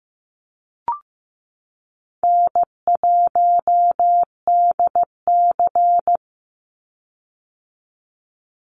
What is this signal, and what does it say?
Morse code, heard as a keyed tone, about 15 wpm, 700 Hz. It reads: N1DC
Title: Easy Call Signs - 15wpm